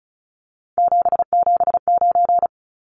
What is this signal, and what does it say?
Morse code, heard as a keyed tone, about 34 wpm, 700 Hz. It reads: 779
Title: Numbers - 3-Digit 35wpm